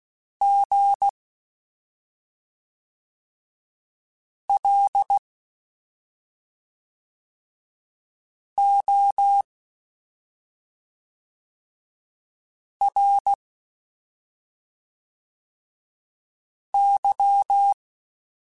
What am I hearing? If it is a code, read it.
Morse code, heard as a keyed tone, about 16 wpm, 800 Hz. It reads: G L O R Y
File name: Special INGs / Special INGs-mp3s / SlowToFaster ing @5